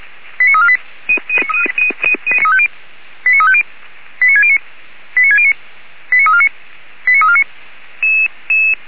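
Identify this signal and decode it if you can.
Five-tone selective call (ZVEI), heard as quick runs of five tones, tones of about 70 ms. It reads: 89348 89350 89609 89609 89349 89349
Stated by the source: ZVEI five-tone call received on SDR, from server of Feuerwehr Orlamünde (fire brigade).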